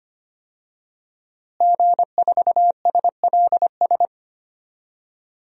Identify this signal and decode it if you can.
Morse code, heard as a keyed tone, about 25 wpm, 700 Hz. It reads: G4SLS
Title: DXCC-100 Call Signs - 5 Characters - Repeated 3x - 25wpm